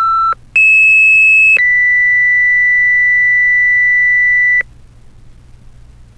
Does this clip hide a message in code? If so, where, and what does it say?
Yes: two-tone page from 0.5 to 4.6 s, heard as a one-second tone, then a three-second tone, about 2600 then 2000 Hz, with noise.